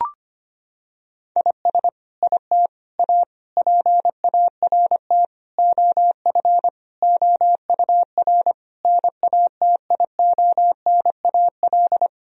Morse code, heard as a keyed tone, about 25 wpm, 700 Hz. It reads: IS IT A PART OF OUR NATIONAL
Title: Sentences from Top 1000 Words - Review Part 4 - 25wpm